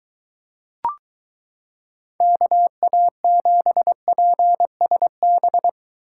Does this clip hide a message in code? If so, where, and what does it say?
Yes: Morse code from 2.2 to 5.7 s, heard as a keyed tone, about 23 wpm, 700 Hz. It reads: KA7PSB